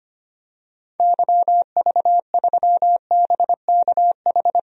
Morse code, heard as a keyed tone, about 25 wpm, 700 Hz. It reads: YV3BKH
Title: DXCC-100 Call Signs - 6 Characters - 25wpm